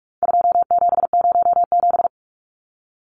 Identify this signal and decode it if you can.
Morse code, heard as a keyed tone, about 44 wpm, 700 Hz. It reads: 2707
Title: Numbers - 4-Digit 45wpm